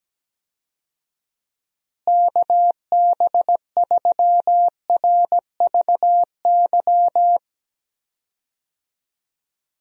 Morse code, heard as a keyed tone, about 17 wpm, 700 Hz. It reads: KB3RVY
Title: USA Call Signs - 6 Characters - Repeated 3x - 17wpm